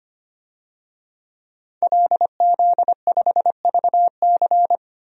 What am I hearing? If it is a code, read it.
Morse code, heard as a keyed tone, about 25 wpm, 700 Hz. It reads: LZ5VC
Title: DXCC-100 Call Signs - 5 Characters - Repeated 3x - 25wpm